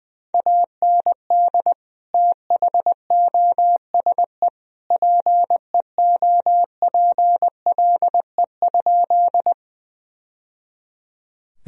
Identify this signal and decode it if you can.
Morse code, heard as a keyed tone, about 21 wpm, 700 Hz. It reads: AND THOSE PEOPLE?
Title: Sentences from Top 1000 Words - Review Part 1 - 20wpm